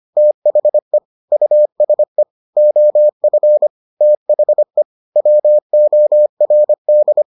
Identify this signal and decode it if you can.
Morse code, heard as a keyed tone, about 25 wpm, 600 Hz. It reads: THE USE OF THE WORD